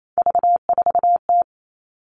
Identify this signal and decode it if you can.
Morse code, heard as a keyed tone, about 28 wpm, 700 Hz. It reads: V4T
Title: All Cut Numbers - 3-Digit 28wpm